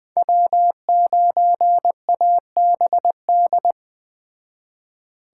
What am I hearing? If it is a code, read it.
Morse code, heard as a keyed tone, about 20 wpm, 700 Hz. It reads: W9ABD